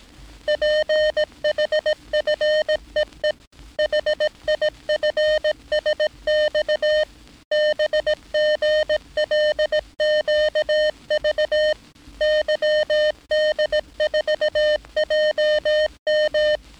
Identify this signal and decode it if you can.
Morse code, heard as a keyed tone, about 18 wpm, 610 Hz. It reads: PHFEE HIFSX BGLQV YD4JM
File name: training / NAVPERS 11049to11060 / NAVPERS 11059 RA RB